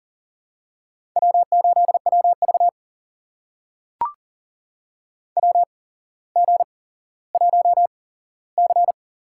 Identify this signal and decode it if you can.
Morse code, heard as a keyed tone, about 38 wpm, spaced as for 17 wpm, 700 Hz. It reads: W8WV WG1C
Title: USA Call Signs - 4 Characters - 8x Character Spacing - 40wpm